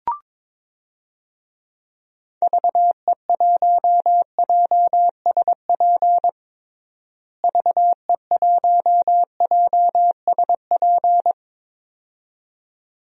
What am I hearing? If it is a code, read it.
Morse code, heard as a keyed tone, about 22 wpm, 700 Hz. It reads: VE1JSP VE1JSP
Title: DXCC-100 Call Signs - 6 Characters - Repeated 2x - 22wpm